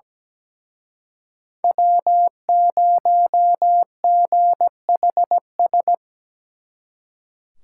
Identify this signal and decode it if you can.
Morse code, heard as a keyed tone, about 20 wpm, 700 Hz. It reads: W0GHS
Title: Speed Racing - USA Call Signs 17wpm